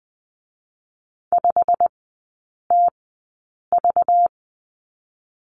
Morse code, heard as a keyed tone, about 20 wpm, spaced as for 8 wpm, 690 Hz. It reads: 5TV
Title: 198 Lesson 32e - Letter k - Three Characters 20wpm